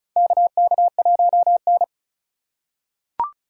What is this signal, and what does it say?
Morse code, heard as a keyed tone, about 36 wpm, 700 Hz. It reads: KK1N
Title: USA Call Signs - 3x Character Spacing - 35wpm